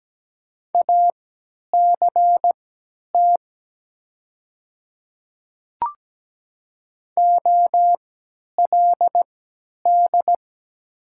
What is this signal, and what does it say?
Morse code, heard as a keyed tone, about 17 wpm, spaced as for 10 wpm, 700 Hz. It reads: ACT OLD
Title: 3x Character Spacing - Top 200 Words - 17wpm